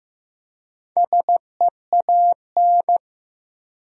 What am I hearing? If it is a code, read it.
Morse code, heard as a keyed tone, about 15 wpm, 700 Hz. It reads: SEAN
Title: USA Common Names - 15wpm